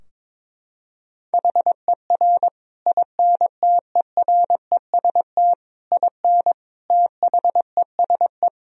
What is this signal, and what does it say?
Morse code, heard as a keyed tone, about 22 wpm, 700 Hz. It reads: HER INTEREST IN THESE
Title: Sentences from Top 1000 Words - Review Part 2 - 22wpm